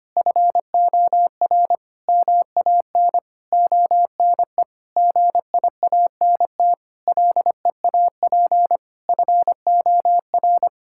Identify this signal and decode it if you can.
Morse code, heard as a keyed tone, about 25 wpm, 700 Hz. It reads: FOR MAN ONE GIANT LEAP FOR
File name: Sets of 9 Words - 25wpm